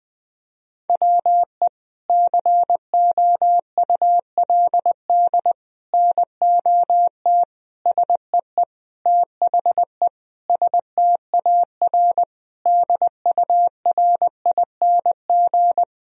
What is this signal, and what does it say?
Morse code, heard as a keyed tone, about 20 wpm, 700 Hz. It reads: WE COULD NOT SEE THE STAR DURING
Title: Sentences from Top 1000 Words - Review Part 2 - 20wpm